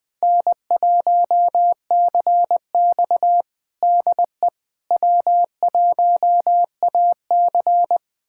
Morse code, heard as a keyed tone, about 20 wpm, 700 Hz. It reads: N1CX DE W1AC
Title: USA Callsign DE Callsign - 4 Characters - 20wpm